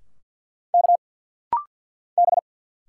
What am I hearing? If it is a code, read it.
Morse code, heard as a keyed tone, about 50 wpm, 690 Hz. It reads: K B
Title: Single Letters - Rapid-Fire 50wpm